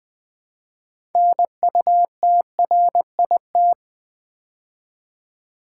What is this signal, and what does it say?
Morse code, heard as a keyed tone, about 20 wpm, 700 Hz. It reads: NUTRIT
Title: Build A Word - 3 Letters At A Time - 9-Letter Words - 20wpm